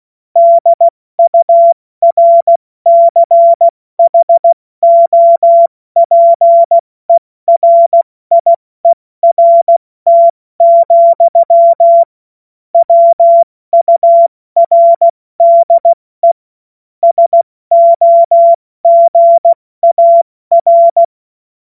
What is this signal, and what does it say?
Morse code, heard as a keyed tone, about 16 wpm, 680 Hz. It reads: DURCHOPERIERT, WURDE SOGAR
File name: Morsekurs / Witze / Witz 02 April 2024 70 F80